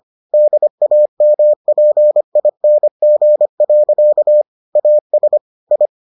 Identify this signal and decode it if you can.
Morse code, heard as a keyed tone, about 25 wpm, 600 Hz. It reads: DAMPING. AS I